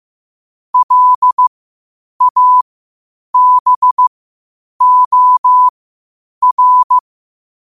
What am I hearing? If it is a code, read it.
Morse code, heard as a keyed tone, about 15 wpm, spaced as for 8 wpm, 1000 Hz. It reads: LABOR
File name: qotdfiles / 8wpm